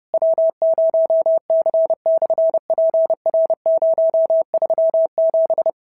Morse code, heard as a keyed tone, about 30 wpm, 650 Hz. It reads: W0C/PR037
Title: SOTA QSO Practice 30wpm